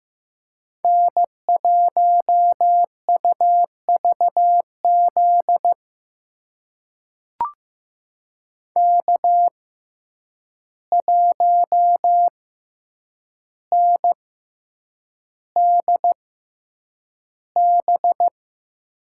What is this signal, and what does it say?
Morse code, heard as a keyed tone, about 15 wpm, spaced as for 8 wpm, 700 Hz. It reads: N1UVZ K1NDB
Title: USA Call Signs - 5 Characters - 6x Character Spacing - 15wpm